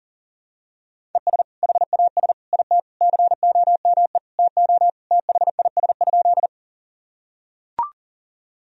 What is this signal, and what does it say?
Morse code, heard as a keyed tone, about 37 wpm, 700 Hz. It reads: ES HAS IT COME TO THIS?
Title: Sentences from Top 100 Words and Abbreviations 40wpm